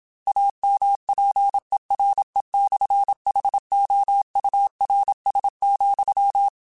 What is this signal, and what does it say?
Morse code, heard as a keyed tone, about 27 wpm, 800 Hz. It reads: AMPERE/HOURS,
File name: Technical Subjects / Technical Subjects-mp3s / Sw Pwr Sup @25